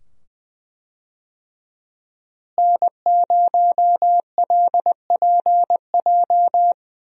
Morse code, heard as a keyed tone, about 21 wpm, 700 Hz. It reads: N0LPJ
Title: USA Call Signs - 5 Characters - Repeated 2x - 20wpm